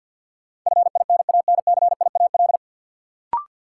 Morse code, heard as a keyed tone, about 52 wpm, 700 Hz. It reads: FINANCIAL
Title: Sentences from Top 1000 Words - Review Part 5 - New Word Drill - 50wpm